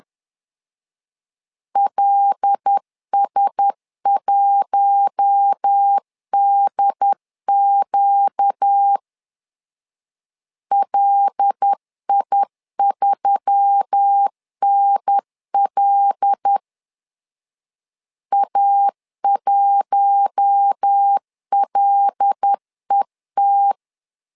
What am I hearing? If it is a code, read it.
Morse code, heard as a keyed tone, about 10 wpm, 800 Hz. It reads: LS1DQ LI3NL A1LET
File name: LEZIONI / 5 LBVC / 5 L 10wpm 2259